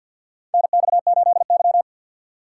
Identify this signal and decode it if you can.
Morse code, heard as a keyed tone, about 47 wpm, 690 Hz. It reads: NX8Y
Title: USA Call Signs - 4 Characters - 6x Character Spacing - 50wpm